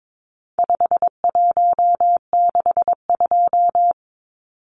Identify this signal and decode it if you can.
Morse code, heard as a keyed tone, about 22 wpm, 700 Hz. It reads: 5162
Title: Numbers - 4-Digit 22wpm